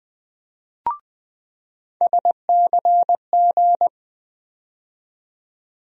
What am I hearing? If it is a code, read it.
Morse code, heard as a keyed tone, about 20 wpm, 700 Hz. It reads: SCG
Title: USA Call Sign Postfixes - 3 Letters 20wpm